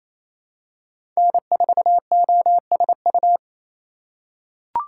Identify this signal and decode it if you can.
Morse code, heard as a keyed tone, about 28 wpm, 700 Hz. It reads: N4OSU
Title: USA Call Signs - 5 Characters - 4x Character Spacing - 28wpm